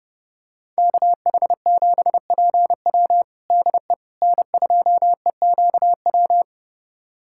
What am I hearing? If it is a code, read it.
Morse code, heard as a keyed tone, about 32 wpm, 700 Hz. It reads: KH7PW DE N2EQW
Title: USA Callsign DE Callsign - 5 Characters - 30wpm